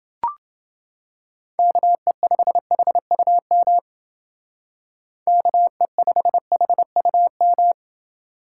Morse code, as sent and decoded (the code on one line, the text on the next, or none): -.- . ..... .... ..- -- / -.- . ..... .... ..- --
KE5HUM KE5HUM